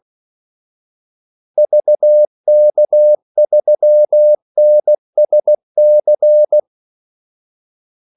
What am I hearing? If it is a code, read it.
Morse code, heard as a keyed tone, about 16 wpm, 600 Hz. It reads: VK3NSC
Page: Kaldesignaler 16 wpm